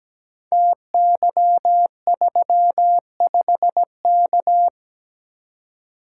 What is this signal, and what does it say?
Morse code, heard as a keyed tone, about 17 wpm, 700 Hz. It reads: TY35K
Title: Code Groups - 5 Letter:Numbers - 17wpm